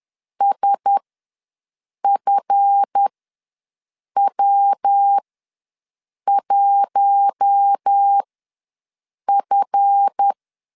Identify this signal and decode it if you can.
Morse code, heard as a keyed tone, about 10 wpm, spaced as for 6 wpm, 800 Hz. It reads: SFW1F
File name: LEZIONI / 7 GF9Z / 7 GF ws 10wpm 3918